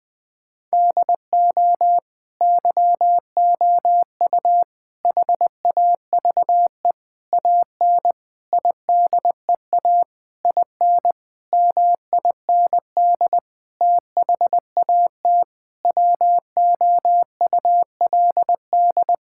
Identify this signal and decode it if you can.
Morse code, heard as a keyed tone, about 21 wpm, 700 Hz. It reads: DO YOU HAVE AN IDEA IN MIND THAT WOULD
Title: Sentences with 9-Word Phrases - Spoken Beginning - 20wpm